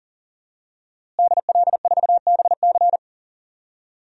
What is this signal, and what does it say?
Morse code, heard as a keyed tone, about 40 wpm, 700 Hz. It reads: DL4BC
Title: DXCC-100 Call Signs - 5 Characters - Repeated 2x - 40wpm